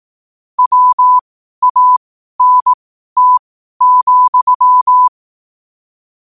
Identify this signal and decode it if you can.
Morse code, heard as a keyed tone, about 18 wpm, spaced as for 12 wpm, 1000 Hz. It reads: WANT,
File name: qotdfiles / 12wpm